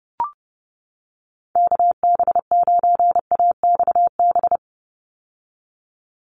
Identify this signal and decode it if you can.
Morse code, heard as a keyed tone, about 30 wpm, 700 Hz. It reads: KB9AXB
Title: USA Call Signs 30wpm